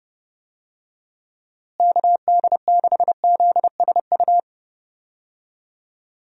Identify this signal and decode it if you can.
Morse code, heard as a keyed tone, about 30 wpm, 700 Hz. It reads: KD6ZSU
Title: USA Call Signs - 6 Characters - Repeated 2x - 30wpm